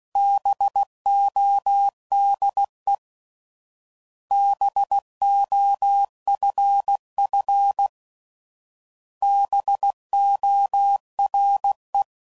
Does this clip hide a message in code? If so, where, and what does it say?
Yes: Morse code from 0.2 to 12.0 s, heard as a keyed tone, about 16 wpm, 800 Hz. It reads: BODE BOFF BORE